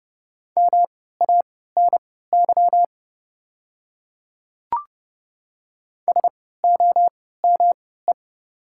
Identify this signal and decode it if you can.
Morse code, heard as a keyed tone, about 30 wpm, spaced as for 17 wpm, 690 Hz. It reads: MANY SOME
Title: 3x Character Spacing - Top 300 Words - 30wpm